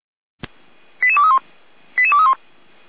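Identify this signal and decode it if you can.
Five-tone selective call (ZVEI), heard as quick runs of five tones, tones of about 70 ms. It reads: 88231 88231
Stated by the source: ZVEI five-tone call received by SDR.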